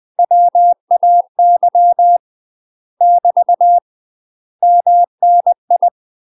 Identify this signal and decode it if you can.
Morse code, heard as a keyed tone, about 20 wpm, 700 Hz. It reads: WAY <BT> MNI
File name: UK MASTER QSOs TRAINING / 20WPM / UK MASTER QSOs 0032